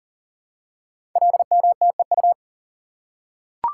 Morse code, heard as a keyed tone, about 38 wpm, 690 Hz. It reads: LMTEU
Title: Code Groups - 5 Letters - 40wpm